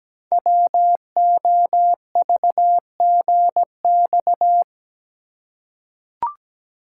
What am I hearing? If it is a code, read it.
Morse code, heard as a keyed tone, about 17 wpm, 700 Hz. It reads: WOVGX
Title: Code Groups - 5 Letters - 17wpm